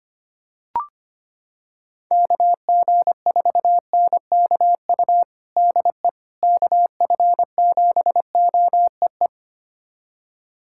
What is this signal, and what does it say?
Morse code, heard as a keyed tone, about 25 wpm, 700 Hz. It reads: KG4NKU DE KF7OEE
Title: USA Callsign DE Callsign - 6 Characters - 25wpm